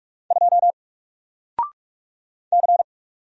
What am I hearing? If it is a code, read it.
Morse code, heard as a keyed tone, about 45 wpm, 700 Hz. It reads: 2 C
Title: Single Character (Letter:Number:Punctuation:Prosign) 45wpm